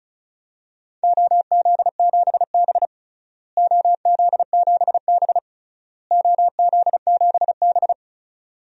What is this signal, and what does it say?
Morse code, heard as a keyed tone, about 36 wpm, 700 Hz. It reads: OZ7B OZ7B OZ7B
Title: DXCC-100 Call Signs - 4 Characters - Repeated 3x - 35wpm